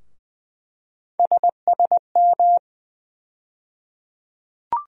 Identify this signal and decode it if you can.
Morse code, heard as a keyed tone, about 20 wpm, 700 Hz. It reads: SSM
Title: USA Call Sign Postfixes - 3 Letters 20wpm